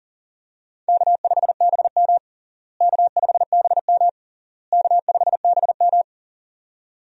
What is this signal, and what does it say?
Morse code, heard as a keyed tone, about 40 wpm, 700 Hz. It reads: K5BM K5BM K5BM
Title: USA Call Signs - 4 Characters - Repeated 3x - 40wpm